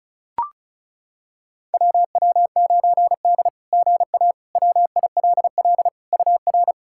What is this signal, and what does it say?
Morse code, heard as a keyed tone, about 36 wpm, 700 Hz. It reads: WW9D GA WILL UR
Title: SOTA QSO Practice 35wpm